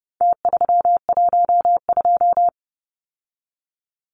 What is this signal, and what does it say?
Morse code, heard as a keyed tone, about 30 wpm, 700 Hz. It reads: T312
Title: Numbers - Contest Style - 4-Digit 30wpm